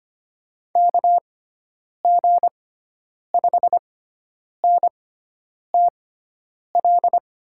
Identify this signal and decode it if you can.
Morse code, heard as a keyed tone, about 25 wpm, spaced as for 9 wpm, 700 Hz. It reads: KG5NTL
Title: USA Call Signs - 6 Characters - 6x Character Spacing - 25wpm